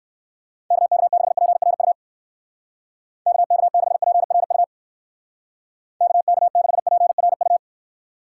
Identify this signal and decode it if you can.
Morse code, heard as a keyed tone, about 55 wpm, 700 Hz. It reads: KK6PRU KK6PRU KK6PRU
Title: Speed Racing - USA 2x3 Call Signs 45wpm